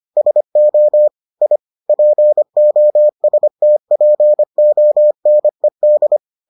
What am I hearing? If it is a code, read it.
Morse code, heard as a keyed tone, about 25 wpm, 600 Hz. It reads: SO I POSTPONED